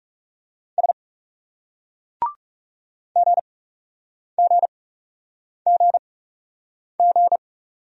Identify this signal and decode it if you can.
Morse code, heard as a keyed tone, about 40 wpm, spaced as for 8 wpm, 700 Hz. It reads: S GGGG